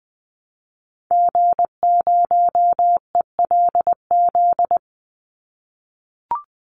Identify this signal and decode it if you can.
Morse code, heard as a keyed tone, about 20 wpm, 700 Hz. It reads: G0ELZ
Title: Call Signs - 20wpm